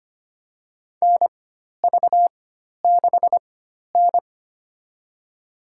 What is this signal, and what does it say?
Morse code, heard as a keyed tone, about 25 wpm, spaced as for 12 wpm, 700 Hz. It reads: NV6N
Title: USA Call Signs - 4x Character Spacing - 25wpm